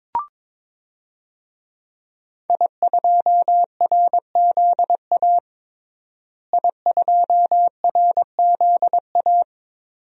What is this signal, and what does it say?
Morse code, heard as a keyed tone, about 22 wpm, 700 Hz. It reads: I2RZA I2RZA
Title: DXCC-100 Call Signs - 5 Characters - Repeated 2x - 22wpm